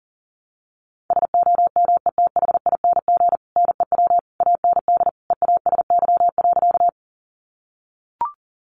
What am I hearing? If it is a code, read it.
Morse code, heard as a keyed tone, about 40 wpm, 700 Hz. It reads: SOMETHING NEW AND EASY.
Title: Sentences from Top 700 Words 40wpm